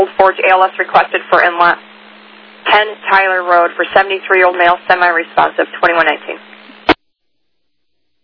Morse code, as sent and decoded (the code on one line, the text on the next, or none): none